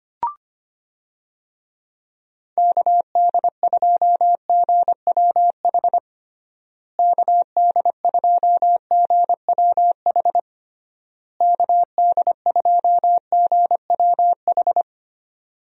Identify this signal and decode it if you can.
Morse code, heard as a keyed tone, about 25 wpm, 700 Hz. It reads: KD2GWH KD2GWH KD2GWH
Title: USA Call Signs - 6 Characters - Repeated 3x - 25wpm